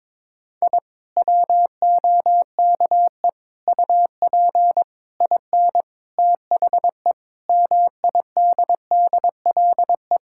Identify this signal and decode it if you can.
Morse code, heard as a keyed tone, about 22 wpm, 700 Hz. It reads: I WOKE UP IN THE MIDDLE
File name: Sentences with 7-Word Phrases - 22wpm